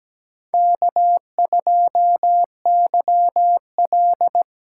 Morse code, heard as a keyed tone, about 17 wpm, 700 Hz. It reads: K2YL
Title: USA Call Signs - 2x Character Spacing - 17wpm